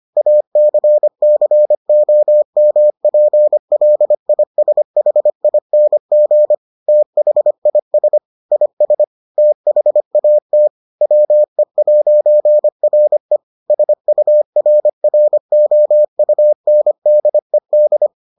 Morse code, heard as a keyed tone, about 25 wpm, 600 Hz. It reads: ACCOMPLISHING THIS IS THAT WE'RE SURROUNDED